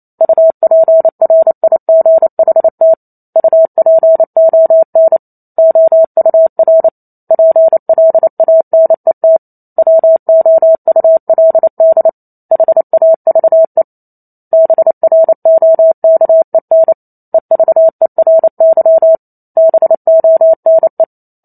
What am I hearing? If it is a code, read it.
Morse code, heard as a keyed tone, about 28 wpm, 650 Hz. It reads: UPRIGHT UPON OUR PLANET WOULD HAVE BROKEN EVERY BONE
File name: Chapter 2 - 28 WPM